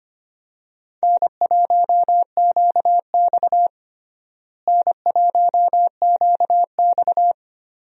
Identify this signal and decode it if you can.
Morse code, heard as a keyed tone, about 25 wpm, 700 Hz. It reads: N1QX N1QX
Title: USA Call Signs - 4 Characters - Repeated 3x - 25wpm